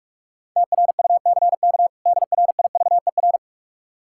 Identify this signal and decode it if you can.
Morse code, heard as a keyed tone, about 46 wpm, 700 Hz. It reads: TRUCK DRIVER
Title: Practice QSO - Exchange 3 - More Info - 45wpm